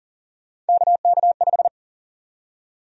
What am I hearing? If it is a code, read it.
Morse code, heard as a keyed tone, about 40 wpm, 700 Hz. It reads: KK5
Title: USA Call Sign Prefixes - 40wpm